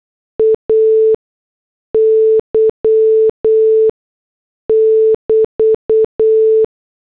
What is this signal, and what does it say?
Morse code, heard as a keyed tone, about 8 wpm, 440 Hz. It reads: A Y <BT>